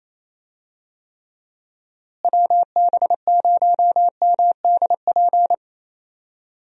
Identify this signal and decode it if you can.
Morse code, heard as a keyed tone, about 28 wpm, 700 Hz. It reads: WB0MDP
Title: USA Call Signs - 6 Characters - Repeated 3x - 28wpm